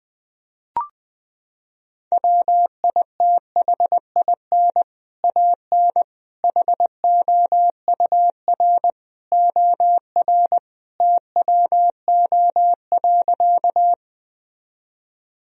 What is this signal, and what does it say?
Morse code, heard as a keyed tone, about 21 wpm, 700 Hz. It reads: WITHIN AN HOUR OR TWO.
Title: Sentences from Top 1000 Words - Review Part 4 - 20wpm